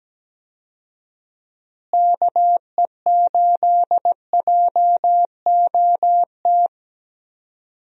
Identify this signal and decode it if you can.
Morse code, heard as a keyed tone, about 17 wpm, 700 Hz. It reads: KE8JOT
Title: USA Call Signs - 6 Characters - Repeated 3x - 17wpm